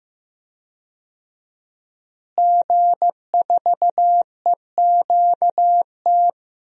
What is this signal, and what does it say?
Morse code, heard as a keyed tone, about 15 wpm, 700 Hz. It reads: G4EQT
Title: DXCC-100 Call Signs - 5 Characters - Repeated 2x - 15wpm